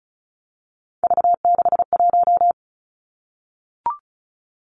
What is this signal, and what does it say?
Morse code, heard as a keyed tone, about 36 wpm, 700 Hz. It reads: V61
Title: Three Challenging Letter-Number Combinations 35wpm